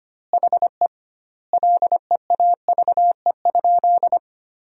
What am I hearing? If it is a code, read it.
Morse code, heard as a keyed tone, about 25 wpm, 700 Hz. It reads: HE LEAVE?
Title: 2x Word-Spacing - Sentences from Top 400 Words - 25wpm